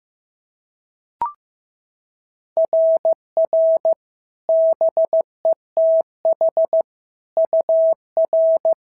Morse code, heard as a keyed tone, about 14 wpm, 650 Hz. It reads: RR BETH UR
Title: SOTA QSO Practice 15wpm